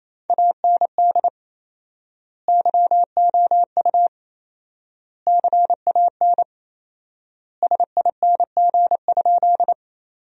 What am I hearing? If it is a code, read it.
Morse code, heard as a keyed tone, about 28 wpm, 700 Hz. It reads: AND YOU CAN SING?
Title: 4x Word-Spacing - Sentences from Top 400 Words - 28wpm